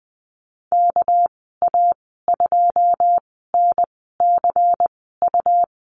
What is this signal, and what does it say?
Morse code, heard as a keyed tone, about 20 wpm, spaced as for 14 wpm, 700 Hz. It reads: KA2NCU
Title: 245 Lesson 39g - Letter z - Call Signs 20wpm